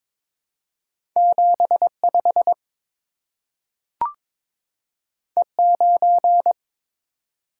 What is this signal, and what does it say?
Morse code, heard as a keyed tone, about 22 wpm, 700 Hz. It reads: 75 E9
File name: practice / 22wpm / Two Letter-Number Combinations - 22wpm